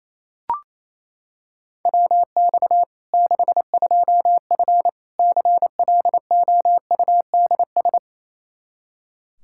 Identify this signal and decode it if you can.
Morse code, heard as a keyed tone, about 28 wpm, 700 Hz. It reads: WX 62F CLOUDS
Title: K1USN SST QSO Practice 28wpm